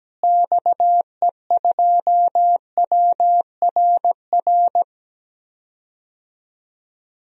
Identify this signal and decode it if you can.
Morse code, heard as a keyed tone, about 17 wpm, 700 Hz. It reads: XE2WRR
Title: DXCC-100 Call Signs - 6 Characters - Repeated 2x - 17wpm